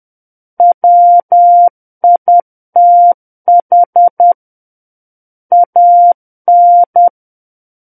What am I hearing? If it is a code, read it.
Morse code, heard as a keyed tone, about 10 wpm, 700 Hz. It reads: WITH AN
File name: Book One / 01-Chapter 1 - 10 WPM